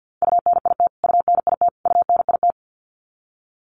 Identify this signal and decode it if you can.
Morse code, heard as a keyed tone, about 50 wpm, 700 Hz. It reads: UNIT UNIT UNIT
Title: Top 500 Words Repeated 50wpm